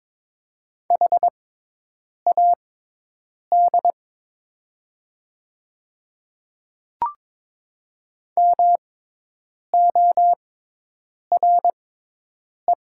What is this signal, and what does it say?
Morse code, heard as a keyed tone, about 22 wpm, spaced as for 8 wpm, 700 Hz. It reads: HAD MORE
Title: 6x Character Spacing - Top 100 Words - 22wpm